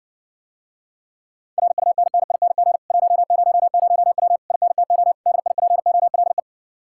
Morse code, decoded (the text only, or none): RUNNING 100W INTO DIPOLE